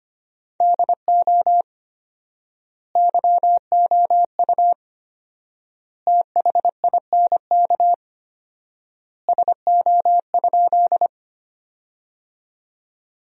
Morse code, heard as a keyed tone, about 25 wpm, 700 Hz. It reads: DO YOU THINK SO?
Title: 4x Word-Spacing - Sentences from Top 200 Words - 25wpm